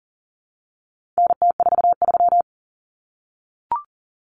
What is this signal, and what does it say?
Morse code, heard as a keyed tone, about 40 wpm, 700 Hz. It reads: NT43
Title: Common Cut Numbers - 4-Digit 40wpm